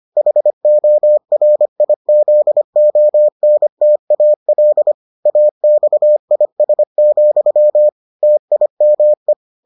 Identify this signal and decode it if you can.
Morse code, heard as a keyed tone, about 25 wpm, 600 Hz. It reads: HORIZONTAL AXIS, TIME